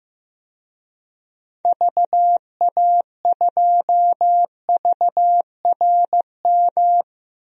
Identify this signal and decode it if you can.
Morse code, heard as a keyed tone, about 16 wpm, 700 Hz. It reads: VA2VRM
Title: DXCC-100 Call Signs - 6 Characters - 15wpm